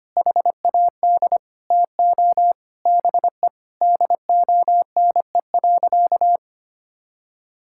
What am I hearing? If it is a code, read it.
Morse code, heard as a keyed tone, about 25 wpm, 700 Hz. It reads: HAD TO BE DONE.
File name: practice / 25wpm / Sentences with 3-Word Phrases - 25wpm